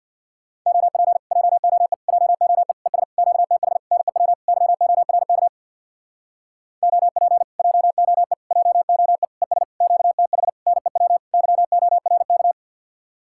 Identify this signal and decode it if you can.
Morse code, heard as a keyed tone, about 62 wpm, 700 Hz. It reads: OP JOE JOE ES QTH NEW YORK OP JOE JOE ES QTH NEW YORK
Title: Speed-Racing Common QSO Phrases - 50wpm